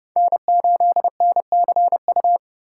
Morse code, heard as a keyed tone, about 30 wpm, 700 Hz. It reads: N8NCU
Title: USA Call Signs - 5 Characters - 2x Character Spacing - 30wpm